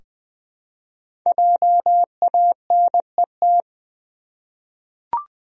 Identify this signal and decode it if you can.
Morse code, heard as a keyed tone, about 20 wpm, 700 Hz. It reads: JANET
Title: USA Common Names - 20wpm